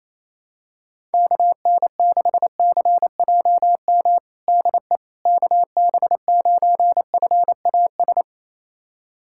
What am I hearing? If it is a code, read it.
Morse code, heard as a keyed tone, about 28 wpm, 700 Hz. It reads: KN6CJM DE KB9FAS
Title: USA Callsign DE Callsign 28wpm